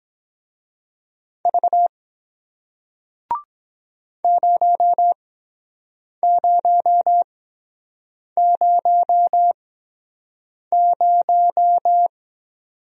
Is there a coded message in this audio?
Yes: Morse code from 1.4 to 12.1 s, heard as a keyed tone, about 23 wpm, spaced as for 7 wpm, 700 Hz. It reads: V 0000